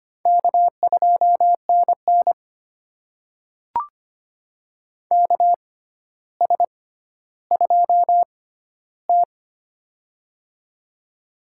Morse code, heard as a keyed tone, about 25 wpm, spaced as for 13 wpm, 700 Hz. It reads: K2NN KS2T
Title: USA Call Signs - 4 Characters - 6x Character Spacing - 25wpm